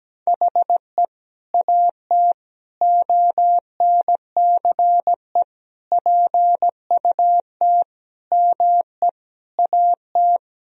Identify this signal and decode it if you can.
Morse code, heard as a keyed tone, about 17 wpm, 700 Hz. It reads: HE AT ONCE PUT ME AT
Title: Sentences from Top 1000 Words - Review Part 1 - 17wpm